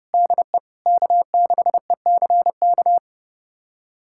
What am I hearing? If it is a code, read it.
Morse code, heard as a keyed tone, about 30 wpm, 700 Hz. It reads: DE K6ECK
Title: USA Callsign DE Callsign - 5 Characters - 30wpm